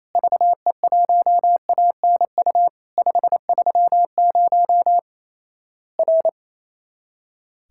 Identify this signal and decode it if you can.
Morse code, heard as a keyed tone, about 28 wpm, 700 Hz. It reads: VE1ANU 530 R
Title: Worked All Europe DX Contest - QTC Exchange - 28wpm